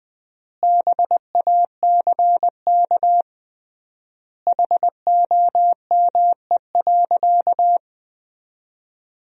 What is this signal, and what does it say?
Morse code, heard as a keyed tone, about 20 wpm, 700 Hz. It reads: BACK HOME.
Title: 3x Word-Spacing - Sentences from Top 500 Words - 20wpm